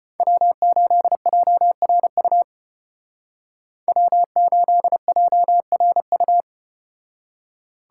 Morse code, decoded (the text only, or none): W8JRU W8JRU